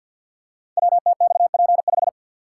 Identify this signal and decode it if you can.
Morse code, heard as a keyed tone, about 50 wpm, 700 Hz. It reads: WTXP5
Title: Code Groups - 5 Letter:Numbers - 50wpm